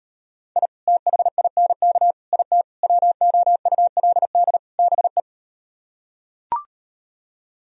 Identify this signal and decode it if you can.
Morse code, heard as a keyed tone, about 38 wpm, 700 Hz. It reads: I THINK IT WOULD BE